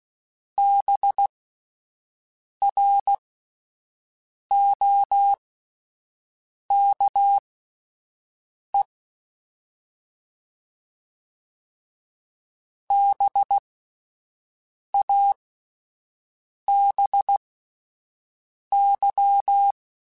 Morse code, heard as a keyed tone, about 16 wpm, spaced as for 5 wpm, 800 Hz. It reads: BROKE BABY